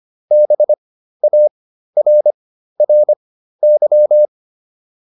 Morse code, heard as a keyed tone, about 25 wpm, spaced as for 13 wpm, 600 Hz. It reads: BARRY